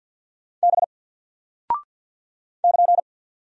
Morse code, heard as a keyed tone, about 48 wpm, 700 Hz. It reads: B <KN>
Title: Single Character (Letter:Number:Punctuation:Prosign) 50wpm